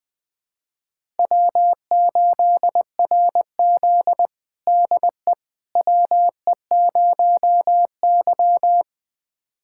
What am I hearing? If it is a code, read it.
Morse code, heard as a keyed tone, about 21 wpm, 700 Hz. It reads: W8RZ DE WE0Y
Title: USA Callsign DE Callsign - 4 Characters - 20wpm